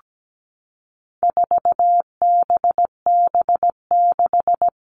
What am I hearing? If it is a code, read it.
Morse code, heard as a keyed tone, about 17 wpm, 700 Hz. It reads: 4BB6
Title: All Cut Numbers - 4-Digit 17wpm